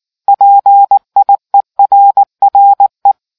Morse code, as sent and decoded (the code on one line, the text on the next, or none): .--. .. .-. .-. .
PIERRE